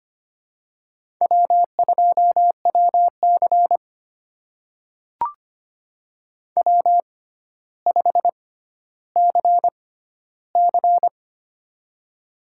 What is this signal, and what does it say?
Morse code, heard as a keyed tone, about 25 wpm, spaced as for 13 wpm, 700 Hz. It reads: W2WC W5CC
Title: USA Call Signs - 4 Characters - 6x Character Spacing - 25wpm